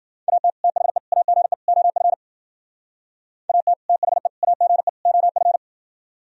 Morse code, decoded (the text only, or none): AT THE AGE OF AT THE AGE OF